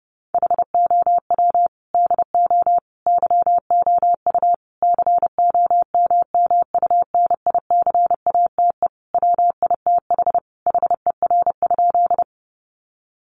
Morse code, heard as a keyed tone, about 30 wpm, 700 Hz. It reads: HOW DO YOU COMMUNICATE WITH HER?
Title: Sentences from Top 1300 Words 30wpm